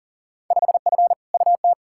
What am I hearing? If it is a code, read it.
Morse code, heard as a keyed tone, about 40 wpm, 700 Hz. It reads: 5F UT
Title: Field Day Exchanges - 40wpm